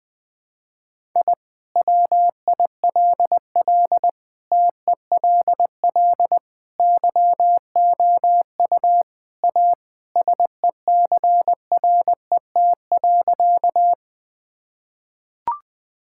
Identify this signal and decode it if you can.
Morse code, heard as a keyed tone, about 20 wpm, 700 Hz. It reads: I WILL TELL YOU A SECRET.
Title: Sentences with 2-Word Phrases - 20wpm